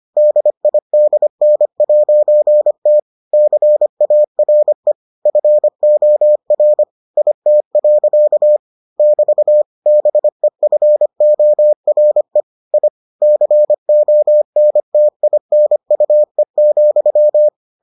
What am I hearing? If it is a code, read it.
Morse code, heard as a keyed tone, about 25 wpm, 600 Hz. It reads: DIDN'T CARE FOR IT. <BT> BEFORE I CONTINUE,